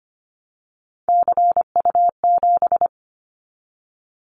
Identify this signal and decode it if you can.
Morse code, heard as a keyed tone, about 25 wpm, 700 Hz. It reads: CU7
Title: Callsign Prefixes for Top 50 DXCC - 25wpm